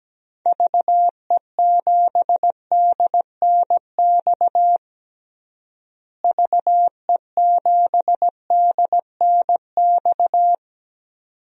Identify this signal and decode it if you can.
Morse code, heard as a keyed tone, about 17 wpm, 700 Hz. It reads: VE7DNX VE7DNX
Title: DXCC-100 Call Signs - 6 Characters - Repeated 2x - 17wpm